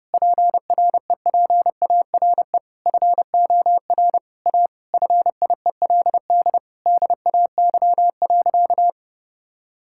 Morse code, heard as a keyed tone, about 30 wpm, 700 Hz. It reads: PREPARE FOR A FIELD DAY.
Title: Sentences from Top 1000 Words - Review Part 6 - 30wpm